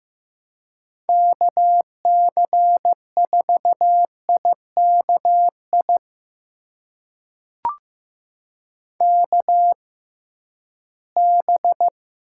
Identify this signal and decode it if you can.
Morse code, heard as a keyed tone, about 15 wpm, spaced as for 11 wpm, 700 Hz. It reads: KC4IKI KB
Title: USA Call Signs - 6 Characters - 6x Character Spacing - 15wpm